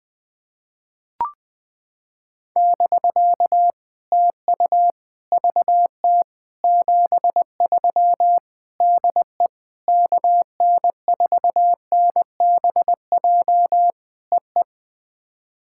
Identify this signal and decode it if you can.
Morse code, heard as a keyed tone, about 21 wpm, 700 Hz. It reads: <BK> TU VT 73 DE KN4NBJ EE
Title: POTA QSO Practice - 20wpm